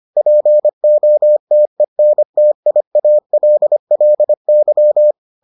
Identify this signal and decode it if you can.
Morse code, heard as a keyed tone, about 25 wpm, 600 Hz. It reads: POTENTIALLY